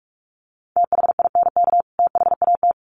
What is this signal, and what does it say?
Morse code, heard as a keyed tone, about 44 wpm, 700 Hz. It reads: THINK THAT
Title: Sets of 2 Words 45wpm